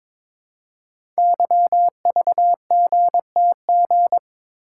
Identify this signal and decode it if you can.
Morse code, heard as a keyed tone, about 22 wpm, 700 Hz. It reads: YVGTG
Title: Code Groups - 5 Letters - 22wpm